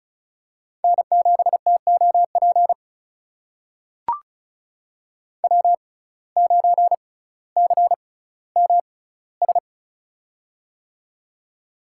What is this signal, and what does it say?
Morse code, heard as a keyed tone, about 36 wpm, spaced as for 18 wpm, 700 Hz. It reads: N7TOP W9CMS
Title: USA Call Signs - 5 Characters - 6x Character Spacing - 35wpm